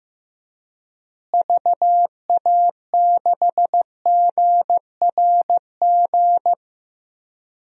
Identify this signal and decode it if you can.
Morse code, heard as a keyed tone, about 15 wpm, 700 Hz. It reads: VA6GRG
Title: DXCC-100 Call Signs - 6 Characters - Repeated 3x - 15wpm